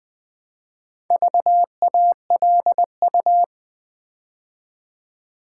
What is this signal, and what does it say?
Morse code, heard as a keyed tone, about 20 wpm, 700 Hz. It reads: VALU
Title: Build A Word - 4 Letters At A Time - 8-Letter Words - 20wpm